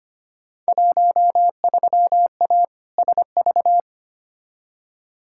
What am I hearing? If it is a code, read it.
Morse code, heard as a keyed tone, about 25 wpm, 700 Hz. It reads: 13A SV
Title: Field Day Exchanges Encore - 25wpm